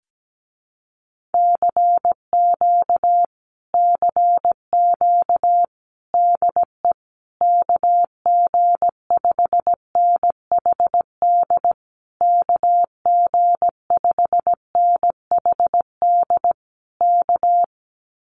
Morse code, heard as a keyed tone, about 17 wpm, 700 Hz. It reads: CQ CQ DE KG5NHD KG5NHD K
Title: Common QSO Phrases Encore 17wpm